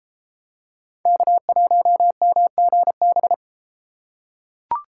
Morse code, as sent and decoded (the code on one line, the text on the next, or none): -.- .---- -- --. -...
K1MGB